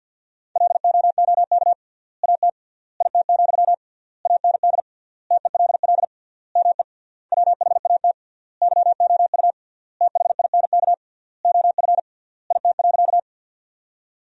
Morse code, heard as a keyed tone, about 50 wpm, 700 Hz. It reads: LOOK AT IT, AND TELL ME WHAT YOU THINK OF IT.
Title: 3x Word-Spacing - Sentences from Top 200 Words - 50wpm